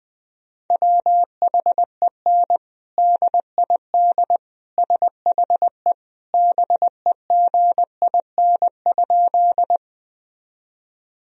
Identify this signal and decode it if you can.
Morse code, heard as a keyed tone, about 20 wpm, 700 Hz. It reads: WHEN DID SHE BEGIN?
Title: Sentences from Top 1000 Words - Review Part 1 - 20wpm